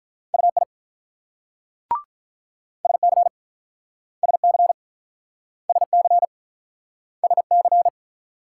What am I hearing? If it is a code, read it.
Morse code, heard as a keyed tone, about 43 wpm, 690 Hz. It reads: UI SC SC SC SC
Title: Speed-Racing Multi-letter Phonemes - 35wpm